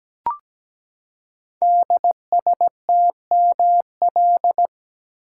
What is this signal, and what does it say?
Morse code, heard as a keyed tone, about 17 wpm, 700 Hz. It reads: DSTML
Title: Code Groups - 5 Letters - 17wpm